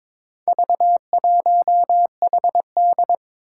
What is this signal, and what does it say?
Morse code, heard as a keyed tone, about 21 wpm, 700 Hz. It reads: V1HD
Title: Four Challenging Letter-Number Combinations 22wpm